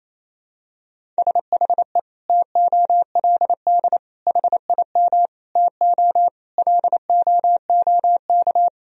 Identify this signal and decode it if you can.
Morse code, heard as a keyed tone, about 28 wpm, 700 Hz. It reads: SHE TOLD HIM TO LOOK
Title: Sentences from Top 1000 Words - Review Part 1 - 28wpm